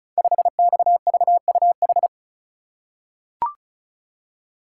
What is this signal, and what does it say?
Morse code, heard as a keyed tone, about 36 wpm, 700 Hz. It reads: 5XVUH
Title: Code Groups - 5 Letter:Numbers - 35wpm